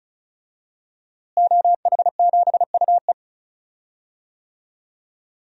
Morse code, heard as a keyed tone, about 36 wpm, 700 Hz. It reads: OH7UE
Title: DXCC-100 Call Signs - 5 Characters - Repeated 3x - 35wpm